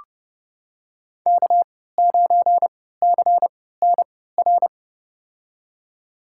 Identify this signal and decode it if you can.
Morse code, heard as a keyed tone, about 30 wpm, spaced as for 17 wpm, 700 Hz. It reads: K9CNR
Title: USA Call Signs - 5 Characters - 3x Character Spacing - 30wpm